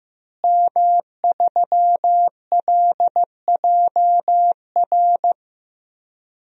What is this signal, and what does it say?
Morse code, heard as a keyed tone, about 15 wpm, 700 Hz. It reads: M3LJR
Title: DXCC-100 Call Signs - 5 Characters - Repeated 3x - 15wpm